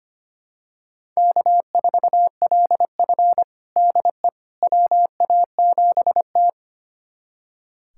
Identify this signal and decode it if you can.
Morse code, heard as a keyed tone, about 25 wpm, 700 Hz. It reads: K4LF DE WA7T
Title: USA Callsign DE Callsign - 4 Characters - 25wpm